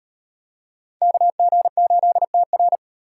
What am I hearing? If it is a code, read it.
Morse code, heard as a keyed tone, about 38 wpm, 700 Hz. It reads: KG8TR